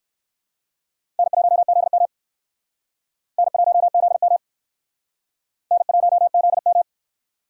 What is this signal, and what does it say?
Morse code, heard as a keyed tone, about 66 wpm, 700 Hz. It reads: N1ZM N1ZM N1ZM